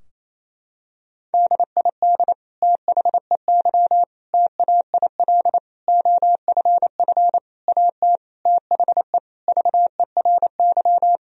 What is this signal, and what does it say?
Morse code, heard as a keyed tone, about 28 wpm, 700 Hz. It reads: DID THEY TAIL OFF AT THE VERY
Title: Sentences from Top 1000 Words - Review Part 3 - 28wpm